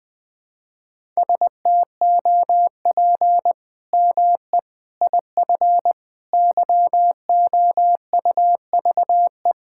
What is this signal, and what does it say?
Morse code, heard as a keyed tone, about 20 wpm, 700 Hz. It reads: STOP ME IF YOUVE
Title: Sets of 8 Words - 20wpm